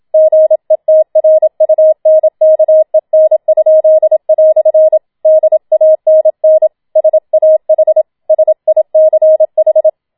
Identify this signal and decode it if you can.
Morse code, heard as a keyed tone, about 27 wpm, 630 Hz. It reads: GETRUNKEN?" DANN SAH SICH
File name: Schneewittchen Teil2 27wpm